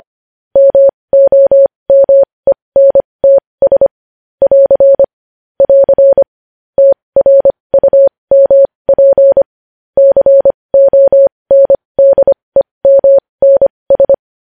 Morse code, heard as a keyed tone, about 25 wpm, spaced as for 20 wpm, 550 Hz. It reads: MOMENTS <AR> <AR> TRUMP CONDEMNS
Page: news-headlines-in-morse-code-at-20-wpm